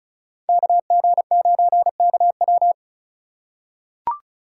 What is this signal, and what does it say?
Morse code, heard as a keyed tone, about 34 wpm, 700 Hz. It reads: KG9KW
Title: USA Call Signs - 5 Characters - 3x Character Spacing - 35wpm